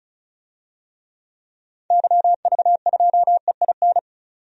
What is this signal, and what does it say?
Morse code, heard as a keyed tone, about 34 wpm, 700 Hz. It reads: YV2EIN
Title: DXCC-100 Call Signs - 6 Characters - 35wpm